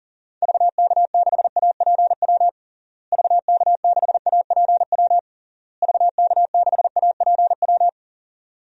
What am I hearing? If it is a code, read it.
Morse code, heard as a keyed tone, about 40 wpm, 700 Hz. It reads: VK6APW VK6APW VK6APW
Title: DXCC-100 Call Signs - 6 Characters - Repeated 3x - 40wpm